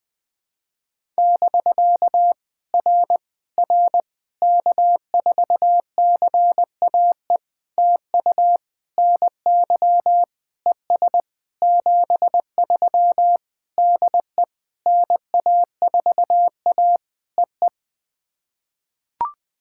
Morse code, heard as a keyed tone, about 21 wpm, 700 Hz. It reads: <BK> R R K4CAE TU NY ES 73 DE NA4A EE
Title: POTA QSO Practice - 20wpm